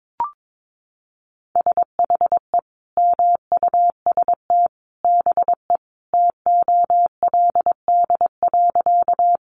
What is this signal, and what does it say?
Morse code, heard as a keyed tone, about 22 wpm, 700 Hz. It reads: SHE MUST BE TOLD.
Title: Sentences from Top 400 Words 22wpm